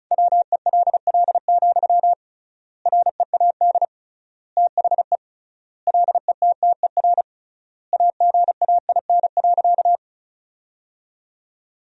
Morse code, heard as a keyed tone, about 36 wpm, 700 Hz. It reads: WELL, READ THE LETTER AGAIN.
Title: 3x Word-Spacing - Sentences from Top 300 Words - 35wpm